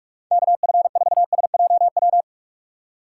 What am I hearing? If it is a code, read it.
Morse code, heard as a keyed tone, about 59 wpm, spaced as for 52 wpm, 700 Hz. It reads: KF4SJW
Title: Speed Racing - USA Call Signs 45wpm